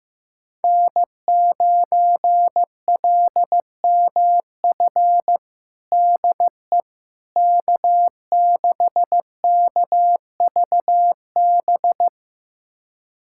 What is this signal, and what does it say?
Morse code, heard as a keyed tone, about 15 wpm, 700 Hz. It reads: N9LMF DE K6KVB
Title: USA Callsign DE Callsign - 5 Characters - 15wpm